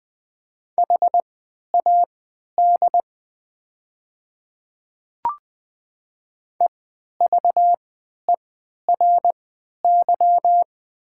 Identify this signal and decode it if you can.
Morse code, heard as a keyed tone, about 20 wpm, spaced as for 11 wpm, 700 Hz. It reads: HAD EVERY
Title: 3x Character Spacing - Top 200 Words - 20wpm